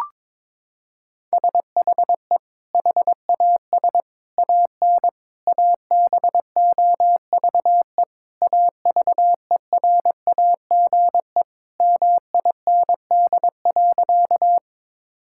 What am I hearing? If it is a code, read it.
Morse code, heard as a keyed tone, about 22 wpm, 700 Hz. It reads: SHE HAS AN ABOVE AVERAGE MIND.
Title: Sentences from Top 1000 Words - Review Part 6 - 22wpm